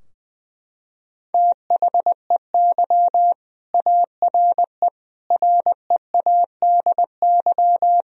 Morse code, heard as a keyed tone, about 20 wpm, 700 Hz. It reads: THEY ARE READY
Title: Sentences from Top 1000 Words - Review Part 3 - 20wpm